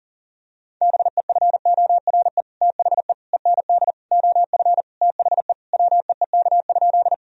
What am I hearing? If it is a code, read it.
Morse code, heard as a keyed tone, about 37 wpm, 700 Hz. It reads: BEFORE THE END OF THE WEEK?
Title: Sentences from Top 1000 Words - Review Part 2 - 40wpm